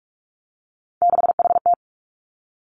Morse code, heard as a keyed tone, about 46 wpm, 700 Hz. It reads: 6HT
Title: Three Challenging Letter-Number Combinations 45wpm